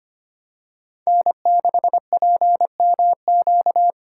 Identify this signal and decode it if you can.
Morse code, heard as a keyed tone, about 25 wpm, 700 Hz. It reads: N6PMQ
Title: USA Call Signs - 3x Character Spacing - 25wpm